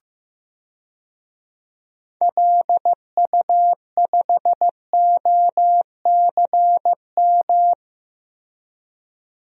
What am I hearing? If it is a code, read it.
Morse code, heard as a keyed tone, about 15 wpm, 700 Hz. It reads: LU5OCM
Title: DXCC-100 Call Signs - 6 Characters - Repeated 2x - 15wpm